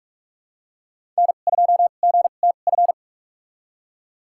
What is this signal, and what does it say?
Morse code, heard as a keyed tone, about 46 wpm, spaced as for 32 wpm, 700 Hz. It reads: N2GTF
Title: USA Call Signs - 5 Characters - 2x Character Spacing - 45wpm